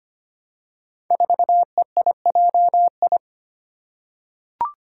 Morse code, heard as a keyed tone, about 25 wpm, 700 Hz. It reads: 4EIJI
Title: Five Challenging Letter-Number Combinations 25wpm